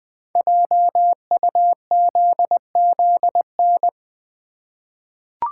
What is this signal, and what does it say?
Morse code, heard as a keyed tone, about 20 wpm, 700 Hz. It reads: JUZZN
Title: Code Groups - 5 Letters - 20wpm